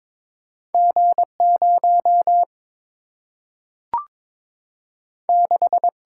Morse code, heard as a keyed tone, about 22 wpm, 700 Hz. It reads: G 0 6
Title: Single-Letter Single-Number 22wpm